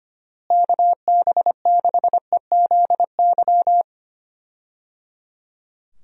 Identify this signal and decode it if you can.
Morse code, heard as a keyed tone, about 25 wpm, 700 Hz. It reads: KB6EZY
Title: USA Call Signs - 6 Characters - Repeated 3x - 25wpm